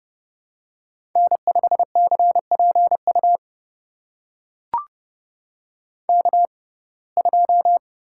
Morse code, heard as a keyed tone, about 30 wpm, spaced as for 22 wpm, 700 Hz. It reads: N5CPU K2
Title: USA Call Signs - 5 Characters - 6x Character Spacing - 30wpm